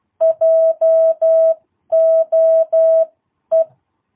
Morse code, heard as a keyed tone, about 12 wpm, spaced as for 10 wpm, 650 Hz. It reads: JOE